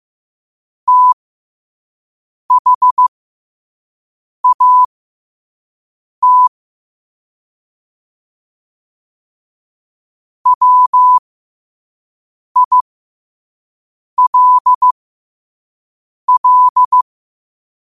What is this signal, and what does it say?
Morse code, heard as a keyed tone, about 15 wpm, spaced as for 5 wpm, 1000 Hz. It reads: THAT WILL
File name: qotdfiles / 5wpm